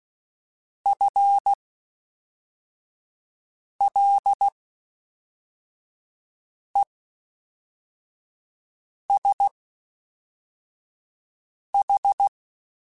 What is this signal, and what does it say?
Morse code, heard as a keyed tone, about 16 wpm, spaced as for 4 wpm, 800 Hz. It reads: FLESH
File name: Spaced / FL Runs sp mp3s / FL Runs sp @5